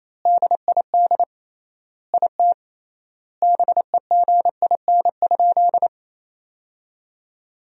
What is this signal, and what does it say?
Morse code, heard as a keyed tone, about 28 wpm, 700 Hz. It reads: DID IT BEGIN?
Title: 3x Word-Spacing - Sentences from Top 300 Words - 28wpm